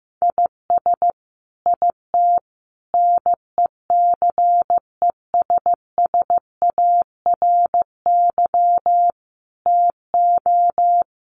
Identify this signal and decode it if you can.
Morse code, heard as a keyed tone, about 15 wpm, 700 Hz. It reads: IS IT NECESSARY TO
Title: Sentences from Top 1500 Words 15wpm